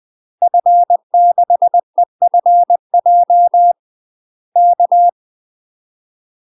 Morse code, heard as a keyed tone, about 20 wpm, 700 Hz. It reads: F6EFJ K
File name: UK MASTER QSOs TRAINING / 20WPM / UK MASTER QSOs 0059